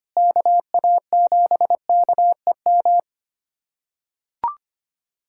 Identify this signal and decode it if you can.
Morse code, heard as a keyed tone, about 25 wpm, 700 Hz. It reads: KA7KEM
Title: USA Callsign DE Callsign - 6 Characters - 25wpm